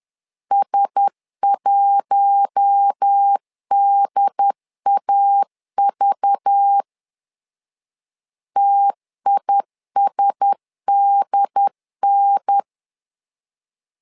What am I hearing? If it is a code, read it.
Morse code, heard as a keyed tone, about 10 wpm, 800 Hz. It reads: S1DAV TISDN
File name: LEZIONI / 5 LBVC / 5 LBV 10wpm 2248